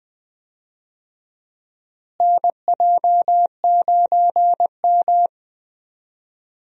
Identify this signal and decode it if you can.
Morse code, heard as a keyed tone, about 20 wpm, 700 Hz. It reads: NJ9M
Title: USA Call Signs - 4 Characters - Repeated 3x - 20wpm